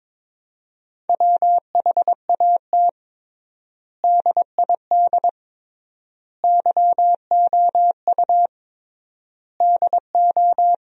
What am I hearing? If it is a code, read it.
Morse code, heard as a keyed tone, about 22 wpm, 700 Hz. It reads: WHAT DID YOU DO
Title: 3x Word-Spacing - Sentences from Top 100 Words - 22wpm